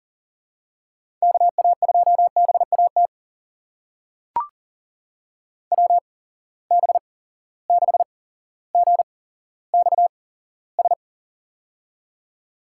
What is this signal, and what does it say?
Morse code, heard as a keyed tone, about 40 wpm, spaced as for 17 wpm, 700 Hz. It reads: KA2BAT WB6GXS
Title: USA Call Signs - 6 Characters - 8x Character Spacing - 40wpm